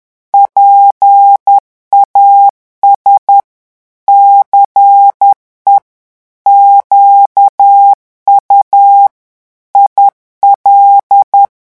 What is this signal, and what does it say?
Morse code, heard as a keyed tone, about 10 wpm, 800 Hz. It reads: PAS CE QU IL
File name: morse mp3 / revision francais 10 mots minute